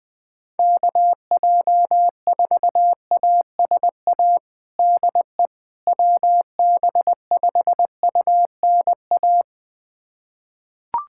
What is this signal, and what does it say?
Morse code, heard as a keyed tone, about 20 wpm, 700 Hz. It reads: KJ4ASA DE WB5UNA
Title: USA Callsign DE Callsign - 6 Characters - 20wpm